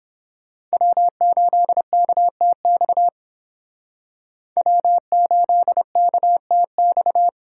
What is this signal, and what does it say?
Morse code, heard as a keyed tone, about 30 wpm, 700 Hz. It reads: W8KTX W8KTX